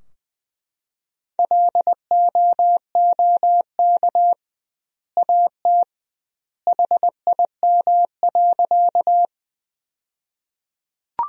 Morse code, heard as a keyed tone, about 21 wpm, 700 Hz. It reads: LOOK AT HIM.
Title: 2x Word-Spacing - Sentences from Top 100 Words - 20wpm